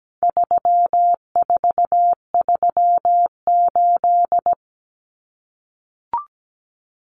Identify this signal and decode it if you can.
Morse code, heard as a keyed tone, about 17 wpm, 700 Hz. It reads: 3438
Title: Numbers - 4-Digit 17wpm